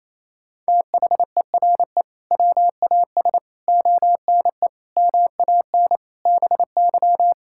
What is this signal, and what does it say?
Morse code, heard as a keyed tone, about 28 wpm, 700 Hz. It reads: THERE WAS ONE MAN BY
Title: Sentences from Top 1600 Words 28wpm